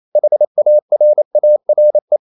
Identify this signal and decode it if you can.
Morse code, heard as a keyed tone, about 28 wpm, 600 Hz. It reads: HARARE
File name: CW / Practice / NationalCapitals28